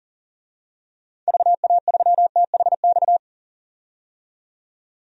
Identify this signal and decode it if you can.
Morse code, heard as a keyed tone, about 40 wpm, 700 Hz. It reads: VA3THX
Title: DXCC-100 Call Signs - 6 Characters - 40wpm